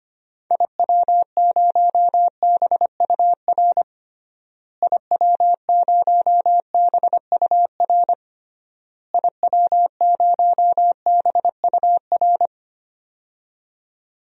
Morse code, heard as a keyed tone, about 25 wpm, 700 Hz. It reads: IW0BUR IW0BUR IW0BUR
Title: DXCC-100 Call Signs - 6 Characters - Repeated 3x - 25wpm